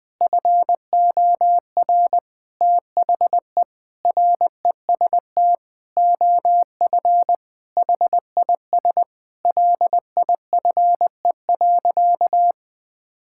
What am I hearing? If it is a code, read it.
Morse code, heard as a keyed tone, about 20 wpm, 700 Hz. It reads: FOR THE REST OF HIS LIFE.
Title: Sentences with 6-Word Phrases - Spoken Beginning - 20wpm